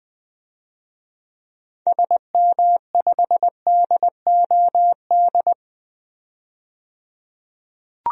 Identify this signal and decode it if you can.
Morse code, heard as a keyed tone, about 20 wpm, 700 Hz. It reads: SM5DOD
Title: DXCC-100 Call Signs - 6 Characters - 20wpm